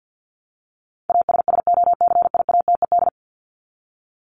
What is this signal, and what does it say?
Morse code, heard as a keyed tone, about 50 wpm, 700 Hz. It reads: ASSOCIATED
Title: ED Suffix - Top 100 Words with Base Words 50wpm